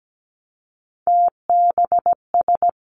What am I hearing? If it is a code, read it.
Morse code, heard as a keyed tone, about 17 wpm, 700 Hz. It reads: TBS
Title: Three Challenging Letter-Number Combinations 17wpm